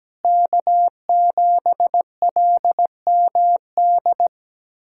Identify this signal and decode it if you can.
Morse code, heard as a keyed tone, about 17 wpm, 700 Hz. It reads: K7LMD
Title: USA Callsign DE Callsign - 5 Characters - 17wpm